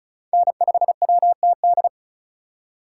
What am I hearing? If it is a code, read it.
Morse code, heard as a keyed tone, about 45 wpm, 700 Hz. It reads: N5WTD